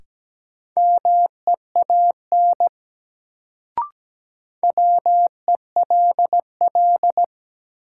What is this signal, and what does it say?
Morse code, heard as a keyed tone, about 17 wpm, 700 Hz. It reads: MEAN WELL
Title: Sentences from Top 200 Words - New Word Drill - 17wpm